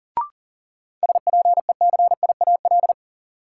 Morse code, heard as a keyed tone, about 40 wpm, 700 Hz. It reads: SPECIAL
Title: Sentences from Top 1000 Words - Review Part 3 - New Word Drill - 40wpm